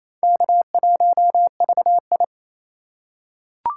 Morse code, heard as a keyed tone, about 28 wpm, 700 Hz. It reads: K1VI
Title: USA Call Signs - 3x Character Spacing - 28wpm